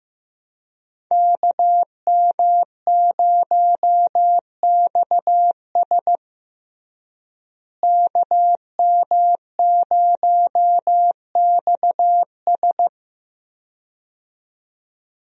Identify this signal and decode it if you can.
Morse code, heard as a keyed tone, about 15 wpm, 700 Hz. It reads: KM0XS KM0XS
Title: USA Call Signs - 5 Characters - Repeated 2x - 15wpm